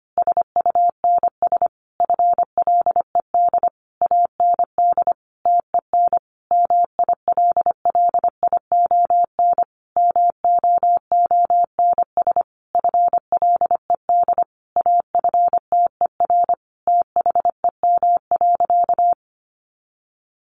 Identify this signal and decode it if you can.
Morse code, heard as a keyed tone, about 25 wpm, 700 Hz. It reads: SUNS FLED AND TEN MILLION MOONS FLED AFTER THEM.
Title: A Sound Of Thunder 25wpm